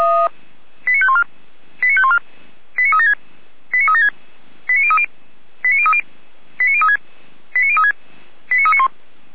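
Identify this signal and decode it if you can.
Five-tone selective call (ZVEI), heard as quick runs of five tones, tones of about 70 ms. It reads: 89514 89514 89376 89376 89030 89030 89036 89036 89391
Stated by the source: Five-tone ZVEI selective call received on SDR.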